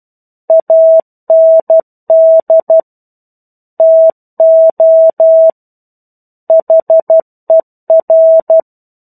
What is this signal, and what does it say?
Morse code, heard as a keyed tone, about 12 wpm, 650 Hz. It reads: AND TO HER
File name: Chapter-01 12 WPM